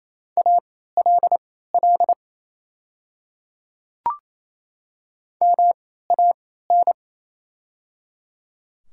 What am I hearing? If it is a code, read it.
Morse code, heard as a keyed tone, about 28 wpm, spaced as for 16 wpm, 690 Hz. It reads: ALL MAN
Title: 3x Character Spacing - Top 500 Words - 28wpm